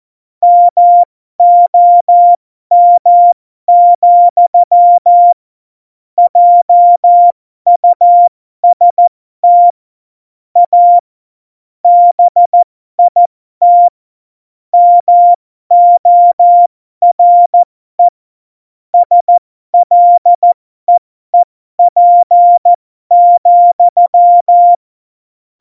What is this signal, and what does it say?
Morse code, heard as a keyed tone, about 14 wpm, spaced as for 12 wpm, 700 Hz. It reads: MOM, JUST A BIT MORE SLEEP,
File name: Morsekurs / Witze